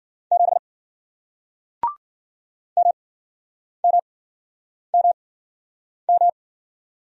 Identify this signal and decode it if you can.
Morse code, heard as a keyed tone, about 55 wpm, 700 Hz. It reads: 7 M M M M